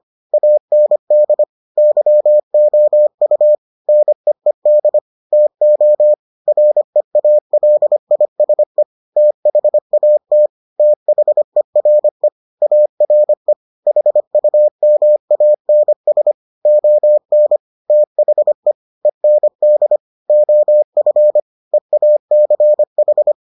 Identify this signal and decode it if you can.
Morse code, heard as a keyed tone, about 25 wpm, 600 Hz. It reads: AND YOU NEED TO REALISE THAT THERE ARE HUMANS ON THE END OF EACH